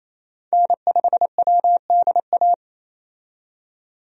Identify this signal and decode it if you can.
Morse code, heard as a keyed tone, about 27 wpm, 700 Hz. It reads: N5WDA
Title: USA Call Signs - 5 Characters - 28wpm